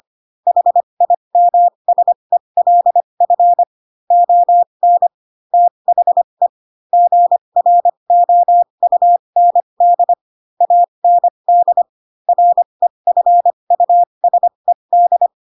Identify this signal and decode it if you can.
Morse code, heard as a keyed tone, about 25 wpm, spaced as for 22 wpm, 700 Hz. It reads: HIMSELF ON THE GROUND AND REFUSED
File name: Pinocchio / Pinocchio ch3 18 25 ik1hge